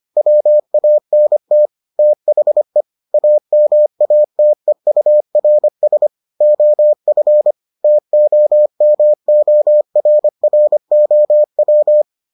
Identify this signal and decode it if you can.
Morse code, heard as a keyed tone, about 25 wpm, 600 Hz. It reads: WANT THE AMATEURS OF TOMORROW